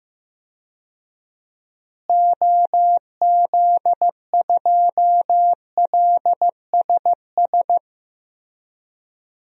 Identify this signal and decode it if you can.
Morse code, heard as a keyed tone, about 15 wpm, 700 Hz. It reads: OZ2LSS
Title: DXCC-100 Call Signs - 6 Characters - Repeated 3x - 15wpm